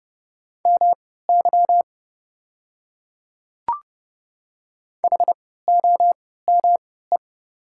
Morse code, heard as a keyed tone, about 30 wpm, spaced as for 17 wpm, 690 Hz. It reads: MY HOME
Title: 3x Character Spacing - Top 300 Words - 30wpm